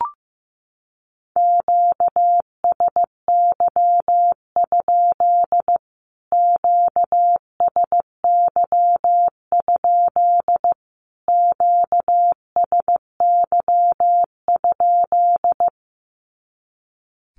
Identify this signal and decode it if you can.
Morse code, heard as a keyed tone, about 15 wpm, 700 Hz. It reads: QSY? QSY? QSY?
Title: Q-Codes Repeated 3x 15wpm